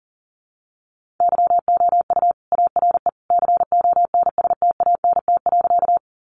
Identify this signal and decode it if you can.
Morse code, heard as a keyed tone, about 40 wpm, 700 Hz. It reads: YOU ARE CONSTANT.
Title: Sentences from Top 1500 Words 40wpm